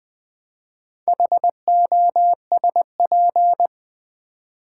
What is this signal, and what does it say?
Morse code, heard as a keyed tone, about 20 wpm, 700 Hz. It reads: HOSP
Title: Build A Word - 4 Letters At A Time - 8-Letter Words - 20wpm